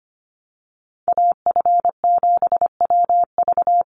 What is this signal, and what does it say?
Morse code, heard as a keyed tone, about 25 wpm, 700 Hz. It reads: AF7WV
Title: USA Call Signs 25wpm